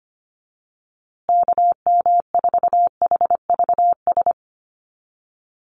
Morse code, heard as a keyed tone, about 25 wpm, 700 Hz. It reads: KM4HVS
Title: USA Call Signs Repeated 2x 25wpm